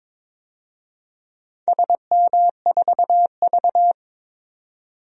Morse code, heard as a keyed tone, about 22 wpm, 700 Hz. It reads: SM4V
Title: DXCC-100 Call Signs - 4 Characters - 22wpm